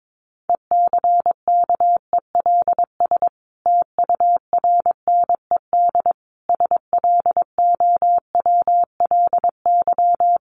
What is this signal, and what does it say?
Morse code, heard as a keyed tone, about 22 wpm, 700 Hz. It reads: ECKELS TURNED SLOWLY
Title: A Sound Of Thunder 22wpm